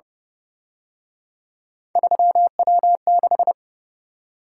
Morse code, heard as a keyed tone, about 30 wpm, 700 Hz. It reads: 3W6